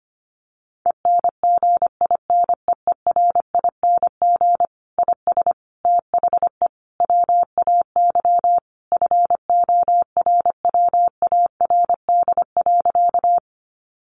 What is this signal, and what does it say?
Morse code, heard as a keyed tone, about 25 wpm, 700 Hz. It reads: ENGINEERING IS THE WAY FORWARD.
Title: Sentences from Top 1500 Words 25wpm